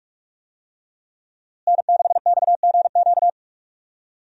Morse code, heard as a keyed tone, about 46 wpm, 700 Hz. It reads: N6XGQ
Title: USA Call Signs - 5 Characters - Repeated 3x - 45wpm